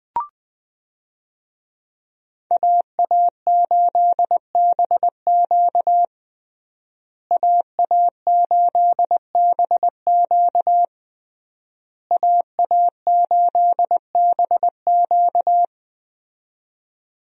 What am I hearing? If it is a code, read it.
Morse code, heard as a keyed tone, about 20 wpm, 700 Hz. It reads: AA8BQ AA8BQ AA8BQ
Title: USA Call Signs - 5 Characters - Repeated 3x - 20wpm